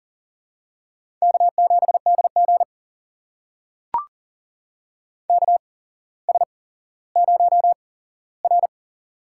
Morse code, heard as a keyed tone, about 37 wpm, spaced as for 17 wpm, 700 Hz. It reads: K7DG KS0R
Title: USA Call Signs - 4 Characters - 8x Character Spacing - 40wpm